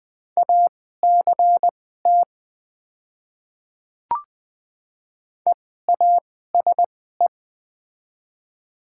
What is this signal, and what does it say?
Morse code, heard as a keyed tone, about 20 wpm, spaced as for 14 wpm, 700 Hz. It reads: ACT EASE
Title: 2x Character Spacing - Top 500 Words - 20wpm